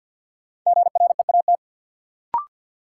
Morse code, heard as a keyed tone, about 47 wpm, 700 Hz. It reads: GREAT
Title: Sentences from Top 200 Words - New Word Drill - 50wpm